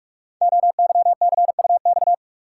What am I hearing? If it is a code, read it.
Morse code, heard as a keyed tone, about 44 wpm, 700 Hz. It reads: OYCUX
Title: Code Groups - 5 Letter:Numbers - 45wpm